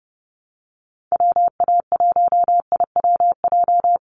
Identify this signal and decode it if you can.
Morse code, heard as a keyed tone, about 30 wpm, 700 Hz. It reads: WA1IWJ
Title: USA Call Signs Repeated 2x 30wpm